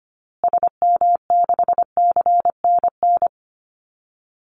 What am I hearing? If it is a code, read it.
Morse code, heard as a keyed tone, about 25 wpm, 700 Hz. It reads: SM6CNN
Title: Call Signs Repeated 2x - 25wpm